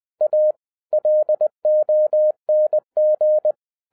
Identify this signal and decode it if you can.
Morse code, heard as a keyed tone, about 20 wpm, 610 Hz. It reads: A LONG
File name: 2011-10-mp3 / n13-TIPS FROM MY 34 YEARS OF EXPERIENCE 20wpm